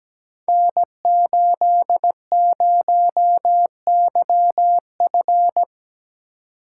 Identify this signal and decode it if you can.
Morse code, heard as a keyed tone, about 17 wpm, 700 Hz. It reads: N80YF
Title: Code Groups - 5 Letter:Numbers - 17wpm